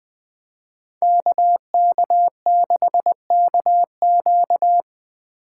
Morse code, heard as a keyed tone, about 20 wpm, 700 Hz. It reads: KK6KQ
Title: Speed Racing - USA 2x2 Call Signs 17wpm